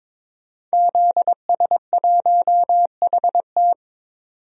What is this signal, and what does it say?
Morse code, heard as a keyed tone, about 22 wpm, 700 Hz. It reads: ZS1HT
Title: DXCC-100 Call Signs - 5 Characters - 22wpm